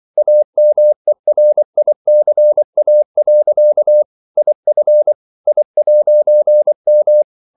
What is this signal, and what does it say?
Morse code, heard as a keyed tone, about 24 wpm, 600 Hz. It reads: AMERICA. IF I'M
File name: CW / Practice / EnglishQuotations0324